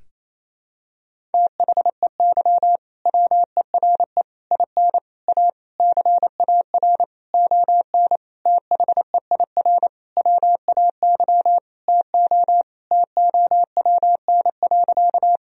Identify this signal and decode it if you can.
Morse code, heard as a keyed tone, about 28 wpm, 700 Hz. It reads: THEY WERE IN A CAR ON THEIR WAY TO TOWN.
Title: Sentences from Top 1000 Words - Review Part 1 - 28wpm